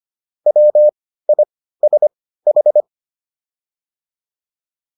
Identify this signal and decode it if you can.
Morse code, heard as a keyed tone, about 25 wpm, spaced as for 15 wpm, 600 Hz. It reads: WISH